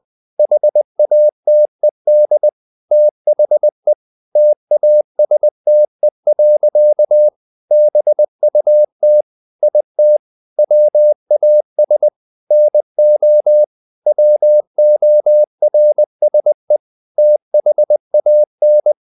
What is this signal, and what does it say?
Morse code, heard as a keyed tone, about 20 wpm, 600 Hz. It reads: HATED THE TASTE. BUT IT WAS NO WORSE THAN
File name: the old man and the sea / the old man and the sea part 1 20wpm